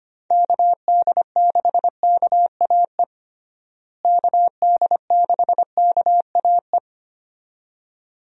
Morse code, heard as a keyed tone, about 25 wpm, 700 Hz. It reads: KD6KAE KD6KAE
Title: USA Call Signs - 6 Characters - Repeated 2x - 25wpm